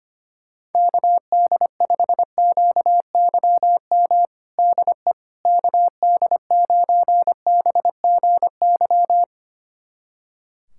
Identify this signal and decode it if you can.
Morse code, heard as a keyed tone, about 25 wpm, 700 Hz. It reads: KD5QYM DE KD9BGY
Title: USA Callsign DE Callsign - 6 Characters - 25wpm